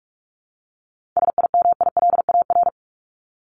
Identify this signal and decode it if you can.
Morse code, heard as a keyed tone, about 46 wpm, 700 Hz. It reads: SIMILAR
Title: Top 501-750 Words 45wpm